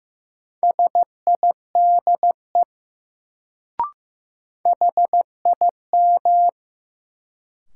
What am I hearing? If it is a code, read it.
Morse code, heard as a keyed tone, about 15 wpm, 700 Hz. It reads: SIDE HIM
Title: Sets of 2 Words - New Word Drill - 15wpm